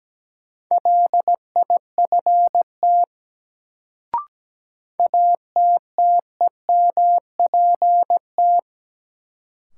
Morse code, heard as a keyed tone, about 17 wpm, 700 Hz. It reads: LIFT ATTEMPT
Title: Sentences from Top 1000 Words - New Word Drill - 17wpm